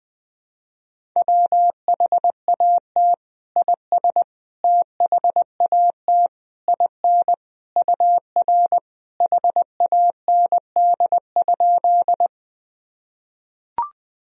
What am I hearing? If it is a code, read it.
Morse code, heard as a keyed tone, about 20 wpm, 700 Hz. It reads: WHAT IS THAT IN UR HAND?
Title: Sentences from Top 200 Words and Abbreviations 20wpm